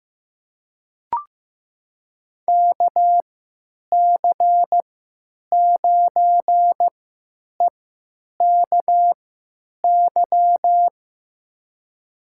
Morse code, heard as a keyed tone, about 15 wpm, spaced as for 9 wpm, 700 Hz. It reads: KC9EKY
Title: USA Call Signs - 6 Characters - 3x Character Spacing - 15wpm